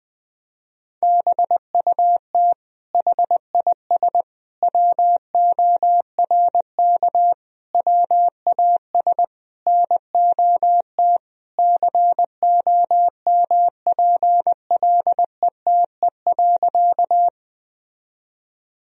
Morse code, heard as a keyed tone, about 20 wpm, 700 Hz. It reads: BUT HIS WORK WAS NOT COMPLETE.
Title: Sentences from Top 1000 Words - Review Part 2 - 20wpm